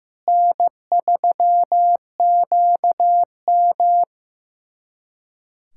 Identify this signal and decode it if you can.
Morse code, heard as a keyed tone, about 15 wpm, 700 Hz. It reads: N3QM
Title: USA Callsign DE Callsign - 4 Characters - 15wpm